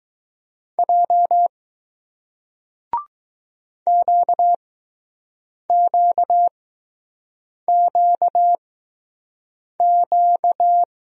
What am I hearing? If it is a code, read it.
Morse code, heard as a keyed tone, about 20 wpm, spaced as for 6 wpm, 700 Hz. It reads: J QQQQ